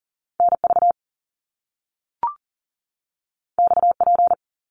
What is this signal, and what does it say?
Morse code, heard as a keyed tone, about 39 wpm, 700 Hz. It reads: NV XP
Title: Top Two Letter Combinations in Words 40wpm